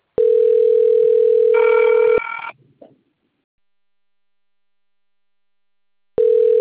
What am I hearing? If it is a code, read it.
Ringback tone (North America), 440 + 480 Hz.